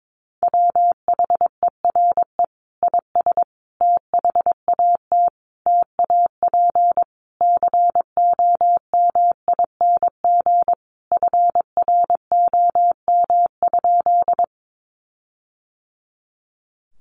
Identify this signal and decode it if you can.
Morse code, heard as a keyed tone, about 22 wpm, 700 Hz. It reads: WHERE IS THAT TAP COMING FROM?
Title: Sentences from Top 1400 Words 22wpm